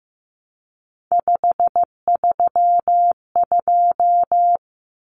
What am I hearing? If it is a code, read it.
Morse code, heard as a keyed tone, about 15 wpm, 700 Hz. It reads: 532
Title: Numbers - 3-Digit 15wpm